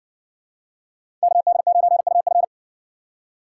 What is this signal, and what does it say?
Morse code, heard as a keyed tone, about 57 wpm, 700 Hz. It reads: KD9UF